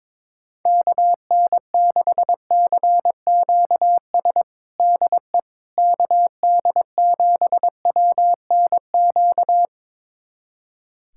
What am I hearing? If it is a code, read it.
Morse code, heard as a keyed tone, about 22 wpm, 700 Hz. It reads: KN6CQS DE KD7WNQ
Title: USA Callsign DE Callsign - 6 Characters - 22wpm